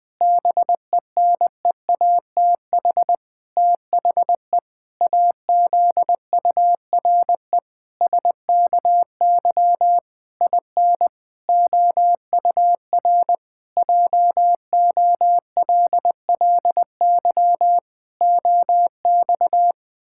Morse code, heard as a keyed tone, about 20 wpm, 700 Hz. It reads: BENEATH THE AZURE SKY IN OUR JOLLY OX
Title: Pangram Challenge v2 - 20wpm